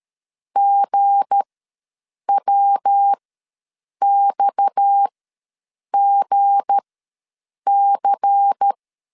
Morse code, heard as a keyed tone, about 13 wpm, spaced as for 7 wpm, 800 Hz. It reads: GWXGC
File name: LEZIONI / 7 GF9Z / 7 G ws 12wpm 3140